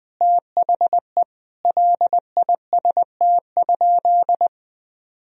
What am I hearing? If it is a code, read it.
Morse code, heard as a keyed tone, about 20 wpm, 700 Hz. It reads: THE LIST?
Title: Sentences from Top 1000 Words - Review Part 2 - 20wpm